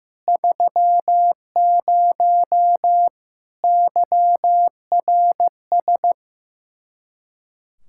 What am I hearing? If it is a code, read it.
Morse code, heard as a keyed tone, about 15 wpm, 700 Hz. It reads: 30 YRS
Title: Speed-Racing Common QSO Phrases Encore - 15wpm